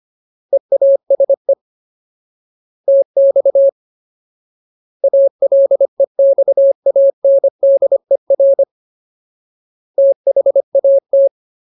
Morse code, heard as a keyed tone, about 25 wpm, 550 Hz. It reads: EASE TX ALEXANDER THAT
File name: Ultimate Word List / 25WPM Ultimate